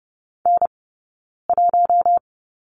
Morse code, heard as a keyed tone, about 30 wpm, 700 Hz. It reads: N 1
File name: Lesson 19d - Letter w - Two Characters 30wpm